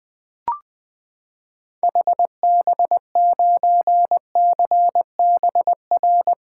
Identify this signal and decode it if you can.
Morse code, heard as a keyed tone, about 21 wpm, 700 Hz. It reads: HB9CBR
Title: 2023 SOTA Activator Call Signs - 20wpm